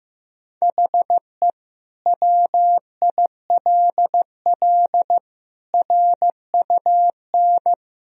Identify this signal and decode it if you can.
Morse code, heard as a keyed tone, about 15 wpm, 700 Hz. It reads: HE WILL RUN
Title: Sentences from Top 1000 Words - Review Part 1 - 15wpm